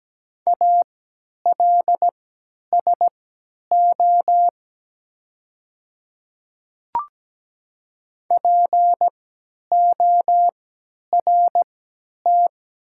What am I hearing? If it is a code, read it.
Morse code, heard as a keyed tone, about 17 wpm, spaced as for 10 wpm, 700 Hz. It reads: ALSO PORT
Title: 3x Character Spacing - Top 200 Words - 17wpm